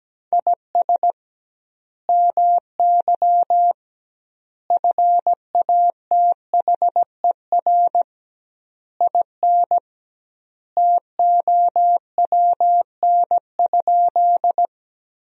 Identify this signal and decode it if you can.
Morse code, heard as a keyed tone, about 17 wpm, 700 Hz. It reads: IS MY FATHER IN TOWN?
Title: 2x Word-Spacing - Sentences from Top 500 Words - 17wpm